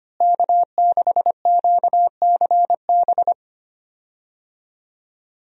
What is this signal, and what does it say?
Morse code, heard as a keyed tone, about 25 wpm, 700 Hz. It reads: K6QCB
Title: USA Call Signs - 5 Characters - Repeated 2x - 25wpm